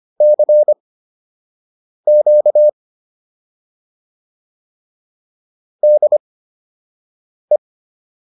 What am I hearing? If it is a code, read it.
Morse code, heard as a keyed tone, about 25 wpm, spaced as for 6 wpm, 600 Hz. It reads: CQ DE